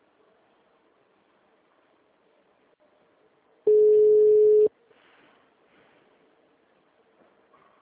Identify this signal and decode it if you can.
Ringback tone (Europe, 425 Hz).